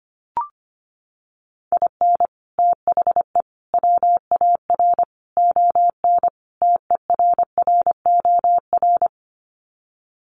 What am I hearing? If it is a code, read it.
Morse code, heard as a keyed tone, about 25 wpm, 700 Hz. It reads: IN THE WAR ON TERROR
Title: Sets of 5 Words Encore2 25wpm